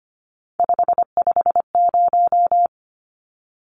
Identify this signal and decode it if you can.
Morse code, heard as a keyed tone, about 25 wpm, 700 Hz. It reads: 550
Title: Numbers - 3-Digit 25wpm